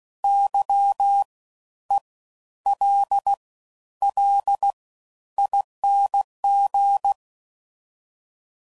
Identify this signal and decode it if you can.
Morse code, heard as a keyed tone, about 16 wpm, spaced as for 10 wpm, 800 Hz. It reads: YELLING
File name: Suffixes / Special INGs / Fast Ings